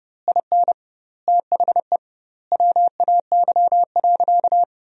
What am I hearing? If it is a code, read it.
Morse code, heard as a keyed tone, about 32 wpm, 700 Hz. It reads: IN THE WAY.
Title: 2x Word-Spacing - Sentences from Top 400 Words - 30wpm